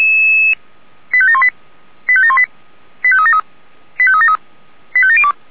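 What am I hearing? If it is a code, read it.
Five-tone selective call (ZVEI), heard as quick runs of five tones, tones of about 70 ms. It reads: 85718 85718 85373 85373 85702
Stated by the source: ZVEI five-tone alert for code 85725, received on an SDR.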